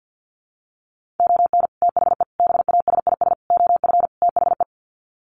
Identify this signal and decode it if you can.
Morse code, heard as a keyed tone, about 50 wpm, 700 Hz. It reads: ON THE BASIS OF THE
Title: Sets of 5 Words 50wpm